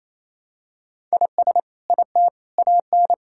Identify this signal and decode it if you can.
Morse code, heard as a keyed tone, about 28 wpm, 700 Hz. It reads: IS IT AN
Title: Sentences from Top 1000 Words - Review Part 5 - 28wpm